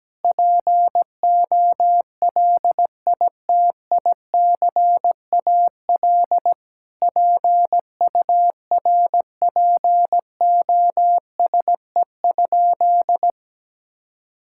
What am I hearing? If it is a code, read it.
Morse code, heard as a keyed tone, about 17 wpm, 700 Hz. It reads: POLITICAL PURPOSE?
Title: Sentences from Top 1000 Words - Review Part 5 - 17wpm